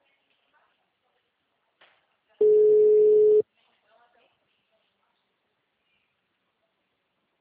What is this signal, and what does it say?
Ringback tone (Europe, 425 Hz), with speech.